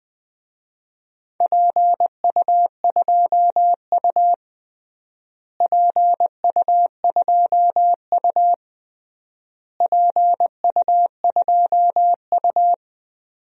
Morse code, heard as a keyed tone, about 20 wpm, 700 Hz. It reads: PU2U PU2U PU2U
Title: DXCC-100 Call Signs - 4 Characters - Repeated 3x - 20wpm